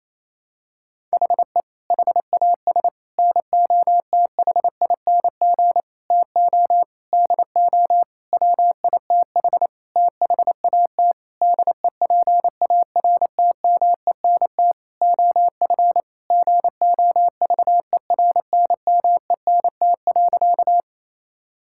Morse code, heard as a keyed tone, about 28 wpm, 700 Hz. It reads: HE HAS NOTHING TO DO WITH THAT DEPARTMENT OF GOVERNMENT.
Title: Sentences from Top 1000 Words - Review Part 5 - 28wpm